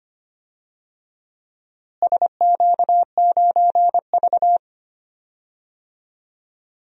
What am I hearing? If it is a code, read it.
Morse code, heard as a keyed tone, about 25 wpm, 700 Hz. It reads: SQ9V
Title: DXCC-100 Call Signs - 4 Characters - Repeated 2x - 25wpm